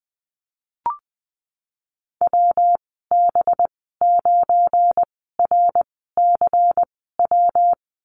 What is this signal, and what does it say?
Morse code, heard as a keyed tone, about 20 wpm, spaced as for 15 wpm, 700 Hz. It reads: WB9RCW
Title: 186 Lesson 30g - Letter b - Call Signs 20wpm